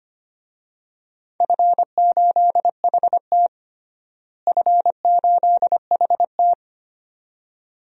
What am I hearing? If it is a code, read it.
Morse code, heard as a keyed tone, about 25 wpm, 700 Hz. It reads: F8HT F8HT
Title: DXCC-100 Call Signs - 4 Characters - Repeated 2x - 25wpm